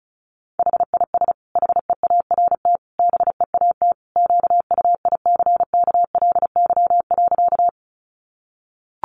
Morse code, heard as a keyed tone, about 34 wpm, 700 Hz. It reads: HIS HEART BEAT QUICKLY.
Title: Sentences from Top 1300 Words 35wpm